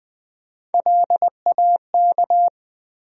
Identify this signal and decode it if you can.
Morse code, heard as a keyed tone, about 20 wpm, 700 Hz. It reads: LAK
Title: USA Call Sign Postfixes - 3 Letters 20wpm